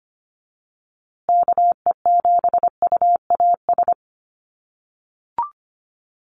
Morse code, heard as a keyed tone, about 25 wpm, 700 Hz. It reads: KE7UAS
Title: USA Call Signs Repeated 3x 25wpm